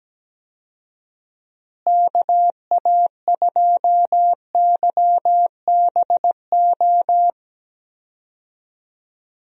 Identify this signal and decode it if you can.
Morse code, heard as a keyed tone, about 17 wpm, 700 Hz. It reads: KA2YBO
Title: USA Call Signs - 6 Characters - Repeated 2x - 17wpm